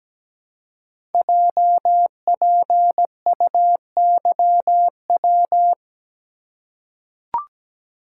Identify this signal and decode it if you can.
Morse code, heard as a keyed tone, about 17 wpm, 700 Hz. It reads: JPUYW
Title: Code Groups - 5 Letters - 17wpm